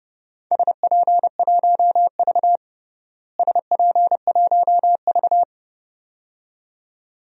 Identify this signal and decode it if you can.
Morse code, heard as a keyed tone, about 30 wpm, 700 Hz. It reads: SP1V SP1V
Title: DXCC-100 Call Signs - 4 Characters - Repeated 2x - 30wpm